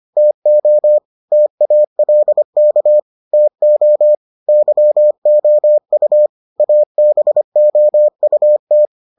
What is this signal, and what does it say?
Morse code, heard as a keyed tone, about 25 wpm, 600 Hz. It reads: TO TALK TO YOU ABOUT